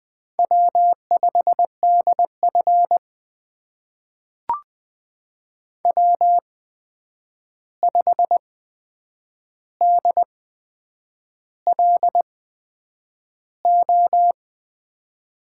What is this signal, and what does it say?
Morse code, heard as a keyed tone, about 21 wpm, spaced as for 8 wpm, 700 Hz. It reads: W5DF W5DLO
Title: USA Call Signs - 8x Character Spacing - 20wpm